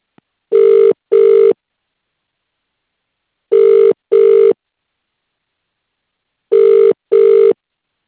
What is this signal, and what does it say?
Ringback tone (UK), 400 + 450 Hz.